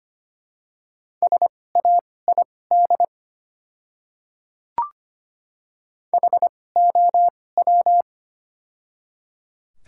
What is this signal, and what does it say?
Morse code, heard as a keyed tone, about 25 wpm, spaced as for 18 wpm, 700 Hz. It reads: SAID HOW
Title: 2x Character Spacing - Top 100 Words - 25wpm